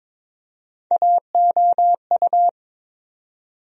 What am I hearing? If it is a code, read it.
Morse code, heard as a keyed tone, about 22 wpm, 700 Hz. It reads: AOU
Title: USA Call Sign Postfixes - 3 Letters 22wpm